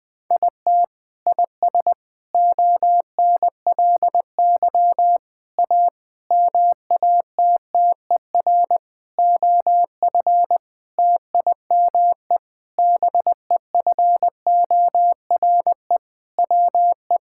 Morse code, heard as a keyed tone, about 20 wpm, 700 Hz. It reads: IT IS ONLY A MATTER OF TIME BEFORE WE
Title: Sentences with 8-Word Phrases - Spoken Beginning - 20wpm